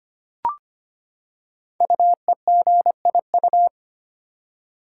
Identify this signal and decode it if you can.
Morse code, heard as a keyed tone, about 25 wpm, 700 Hz. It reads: UEGIU
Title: Code Groups - 5 Letters - 25wpm